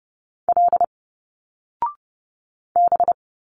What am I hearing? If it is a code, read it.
Morse code, heard as a keyed tone, about 30 wpm, 700 Hz. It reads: L B
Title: Single Letters 30wpm